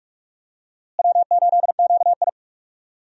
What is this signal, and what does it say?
Morse code, heard as a keyed tone, about 43 wpm, 700 Hz. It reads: W8QI
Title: USA Call Signs - 4 Characters - Repeated 3x - 45wpm